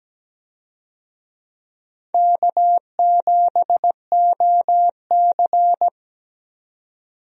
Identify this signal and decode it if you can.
Morse code, heard as a keyed tone, about 17 wpm, 700 Hz. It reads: K7OC
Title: USA Call Signs - 4 Characters - 17wpm